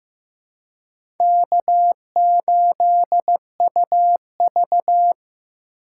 Morse code, heard as a keyed tone, about 15 wpm, 700 Hz. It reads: K8UV
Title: USA Call Signs - 2x Character Spacing - 15wpm